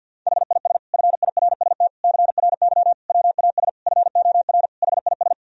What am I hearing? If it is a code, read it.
Morse code, heard as a keyed tone, about 50 wpm, 700 Hz. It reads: HIS FIRST CRY WAS FOR HIS
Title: Sentences from Top 1000 Words - Review Part 2 - 50wpm